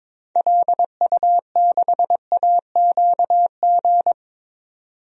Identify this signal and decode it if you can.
Morse code, heard as a keyed tone, about 22 wpm, 700 Hz. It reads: LU6AQG
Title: DXCC-100 Call Signs - 6 Characters - 22wpm